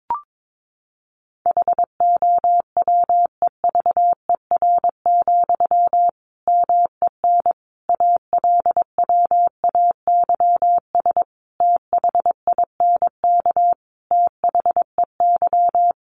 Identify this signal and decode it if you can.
Morse code, heard as a keyed tone, about 22 wpm, 700 Hz. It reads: HOWEVER, MEN ALWAYS THINK THEY
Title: Sentences from Top 600 Words 22wpm